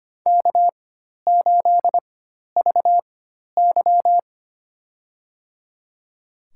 Morse code, heard as a keyed tone, about 25 wpm, spaced as for 12 wpm, 700 Hz. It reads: K8VY
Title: USA Call Signs - 4 Characters - 4x Character Spacing - 25wpm